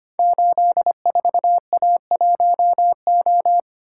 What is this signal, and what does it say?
Morse code, heard as a keyed tone, about 25 wpm, 700 Hz. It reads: 84A1O
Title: Code Groups - 5 Letter:Numbers - 25wpm